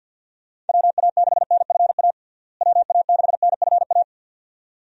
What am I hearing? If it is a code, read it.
Morse code, heard as a keyed tone, about 52 wpm, 700 Hz. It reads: WA6NFA WA6NFA
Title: USA Call Signs - 6 Characters - Repeated 2x - 50wpm